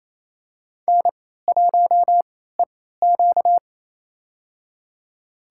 Morse code, heard as a keyed tone, about 28 wpm, spaced as for 16 wpm, 700 Hz. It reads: N1EQ
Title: USA Call Signs - 4 Characters - 3x Character Spacing - 28wpm